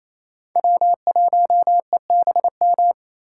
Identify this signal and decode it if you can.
Morse code, heard as a keyed tone, about 28 wpm, 700 Hz. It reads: W1EBM
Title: K1USN SST QSO Practice - 28wpm